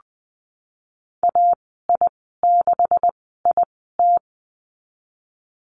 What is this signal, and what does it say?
Morse code, heard as a keyed tone, about 20 wpm, spaced as for 15 wpm, 700 Hz. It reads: AI6IT
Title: 193 Lesson 31g - Letter v - Call Signs 20wpm